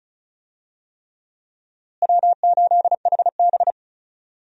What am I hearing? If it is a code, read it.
Morse code, heard as a keyed tone, about 36 wpm, 700 Hz. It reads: W8HB
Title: USA Call Signs - 4 Characters - Repeated 3x - 35wpm